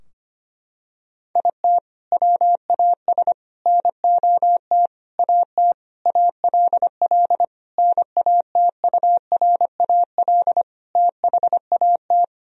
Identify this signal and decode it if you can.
Morse code, heard as a keyed tone, about 25 wpm, 700 Hz. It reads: IT WAS NOT AT ALL NATURAL THAT
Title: Sentences from Top 1000 Words - Review Part 4 - 25wpm